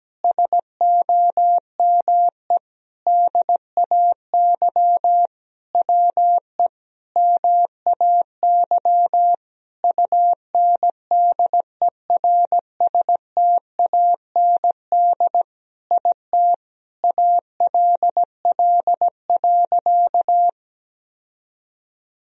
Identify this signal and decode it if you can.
Morse code, heard as a keyed tone, about 17 wpm, 700 Hz. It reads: SOME DAY WE MAY UNDERSTAND IT ALL.
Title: Sentences from Top 1000 Words - Review Part 4 - 17wpm